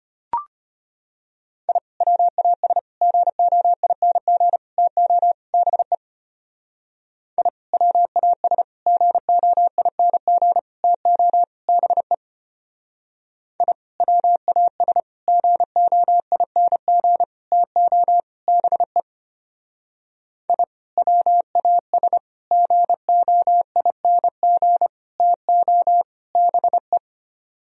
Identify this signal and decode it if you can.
Morse code, heard as a keyed tone, about 32 wpm, 700 Hz. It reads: I WAS GOING TO BE I WAS GOING TO BE I WAS GOING TO BE I WAS GOING TO BE